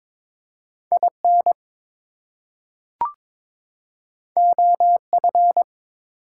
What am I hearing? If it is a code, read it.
Morse code, heard as a keyed tone, about 22 wpm, 700 Hz. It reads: IN OF
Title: Two Letter Words - 22wpm